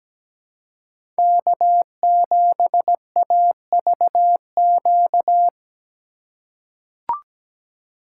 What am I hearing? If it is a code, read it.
Morse code, heard as a keyed tone, about 17 wpm, 700 Hz. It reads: K7AVQ
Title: USA Call Signs - 5 Characters - 2x Character Spacing - 17wpm